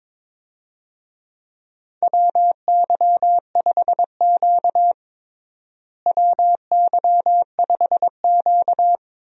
Morse code, heard as a keyed tone, about 22 wpm, 700 Hz. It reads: WY5Q WY5Q
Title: USA Call Signs - 4 Characters - Repeated 2x - 22wpm